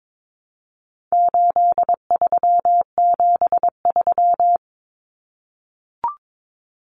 Morse code, heard as a keyed tone, about 22 wpm, 700 Hz. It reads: 8373
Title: Numbers - 4-Digit 22wpm